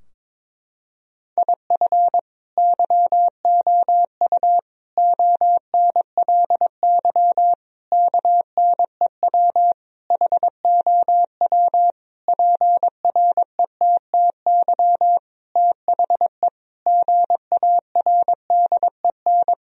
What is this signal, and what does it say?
Morse code, heard as a keyed tone, about 22 wpm, 700 Hz. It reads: IF YOU ONLY KNEW HOW PRETTY THE GARDEN
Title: Sentences from Top 1000 Words - Review Part 4 - 22wpm